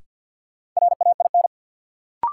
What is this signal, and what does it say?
Morse code, heard as a keyed tone, about 53 wpm, 700 Hz. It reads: RAIN
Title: Sentences from Top 600 Words - New Word Drill - 50wpm